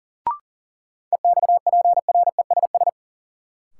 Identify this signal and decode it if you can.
Morse code, heard as a keyed tone, about 40 wpm, 700 Hz. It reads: EXPRESS
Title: Sentences from Top 1100 Words - New Word Drill - 40wpm